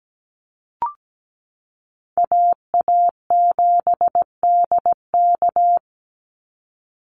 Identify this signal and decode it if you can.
Morse code, heard as a keyed tone, about 17 wpm, 700 Hz. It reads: AA7DK
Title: NA SOTA Chaser Callsigns 17wpm